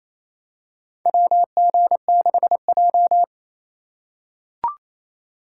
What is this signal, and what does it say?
Morse code, heard as a keyed tone, about 28 wpm, 700 Hz. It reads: WG6J
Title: USA Call Signs - 4x Character Spacing - 28wpm